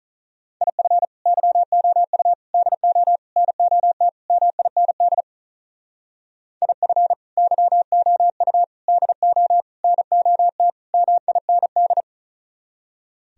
Morse code, heard as a keyed tone, about 44 wpm, 700 Hz. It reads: IF YOU DO NOT MIND IF YOU DO NOT MIND
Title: Speed-Racing - Sets of 5 Words 35wpm